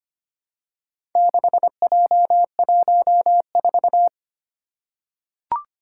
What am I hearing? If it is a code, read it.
Morse code, heard as a keyed tone, about 25 wpm, 700 Hz. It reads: 6J14
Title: Four Challenging Letter-Number Combinations 25wpm